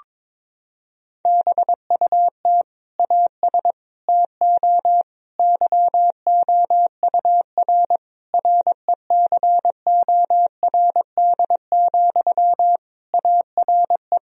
Morse code, heard as a keyed tone, about 22 wpm, 700 Hz. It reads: BUT AS TO YOUR RECORD, ARE
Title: Sentences from Top 1000 Words - Review Part 3 - 22wpm